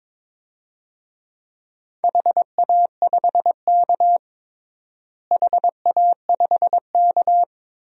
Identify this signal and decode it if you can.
Morse code, heard as a keyed tone, about 23 wpm, 700 Hz. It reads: HA5K HA5K
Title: DXCC-100 Call Signs - 4 Characters - Repeated 2x - 22wpm